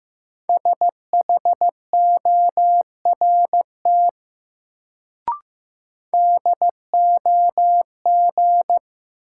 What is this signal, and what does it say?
Morse code, heard as a keyed tone, about 15 wpm, 700 Hz. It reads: SHORT DOG
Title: Sentences from Top 400 Words - New Word Drill - 15wpm